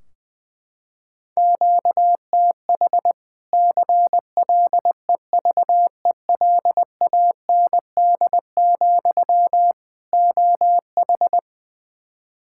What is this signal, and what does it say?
Morse code, heard as a keyed tone, about 20 wpm, 700 Hz. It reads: QTH CLEVELAND, OH
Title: Practice QSO - Exchange 1 - Beginning - 20wpm